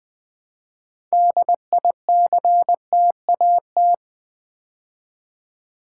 Morse code, heard as a keyed tone, about 20 wpm, 700 Hz. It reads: DICTAT
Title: Build A Word - 3 Letters At A Time - 8-Letter Words - 20wpm